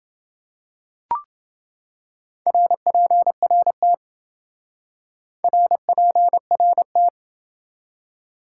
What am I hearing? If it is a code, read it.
Morse code, heard as a keyed tone, about 28 wpm, 700 Hz. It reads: RPRT RPRT